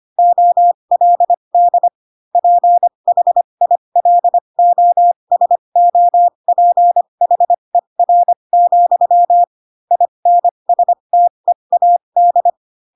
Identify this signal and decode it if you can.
Morse code, heard as a keyed tone, about 25 wpm, spaced as for 22 wpm, 700 Hz. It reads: OLD PHILOSOPHER, INSTEAD
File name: Pinocchio / Pinocchio ch4 18 25 ik1hge